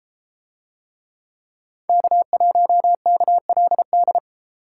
Morse code, heard as a keyed tone, about 32 wpm, 700 Hz. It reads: K1KLD